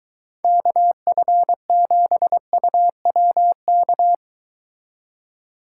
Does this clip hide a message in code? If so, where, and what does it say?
Yes: Morse code from 0.4 to 4.2 s, heard as a keyed tone, about 23 wpm, 700 Hz. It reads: KF7UWK